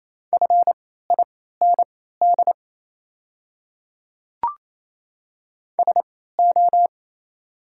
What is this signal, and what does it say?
Morse code, heard as a keyed tone, about 27 wpm, spaced as for 16 wpm, 690 Hz. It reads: FIND SO
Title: 3x Character Spacing - Top 200 Words - 28wpm